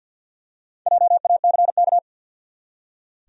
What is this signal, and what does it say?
Morse code, heard as a keyed tone, about 46 wpm, 700 Hz. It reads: JACK
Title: USA Common Names - 50wpm